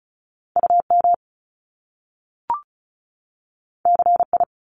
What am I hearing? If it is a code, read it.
Morse code, heard as a keyed tone, about 36 wpm, 700 Hz. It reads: UM CI
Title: Top Two Letter Combinations in Words 35wpm